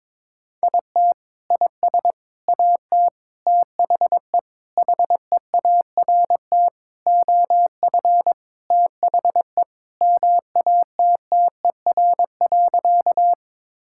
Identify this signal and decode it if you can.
Morse code, heard as a keyed tone, about 22 wpm, 700 Hz. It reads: IT IS AT THE HEART OF THE MATTER.
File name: practice / 22wpm / Sentences with 4-Word Phrases - 22wpm